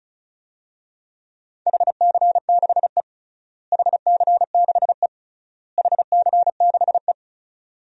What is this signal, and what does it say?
Morse code, heard as a keyed tone, about 34 wpm, 700 Hz. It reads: HC6E HC6E HC6E
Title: DXCC-100 Call Signs - 4 Characters - Repeated 3x - 35wpm